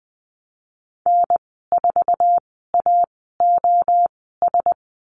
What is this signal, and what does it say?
Morse code, heard as a keyed tone, about 20 wpm, spaced as for 15 wpm, 700 Hz. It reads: N4AOS
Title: 055 Lesson 10g - Letter r - Call Signs 20wpm